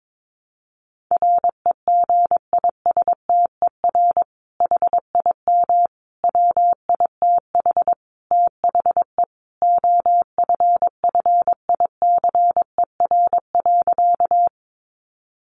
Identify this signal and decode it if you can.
Morse code, heard as a keyed tone, about 22 wpm, 700 Hz. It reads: REGISTER HIM WITH THE OFFICER.
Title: Sentences from Top 1200 Words 22wpm